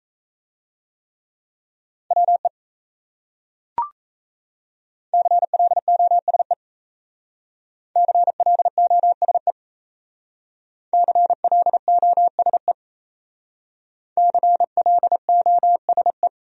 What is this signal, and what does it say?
Morse code, heard as a keyed tone, about 35 wpm, 700 Hz. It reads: WE CLOSE CLOSE CLOSE CLOSE